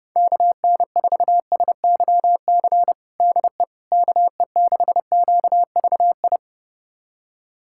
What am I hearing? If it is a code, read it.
Morse code, heard as a keyed tone, about 32 wpm, 700 Hz. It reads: KN4SYC DE KE6QVI
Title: USA Callsign DE Callsign - 6 Characters - 30wpm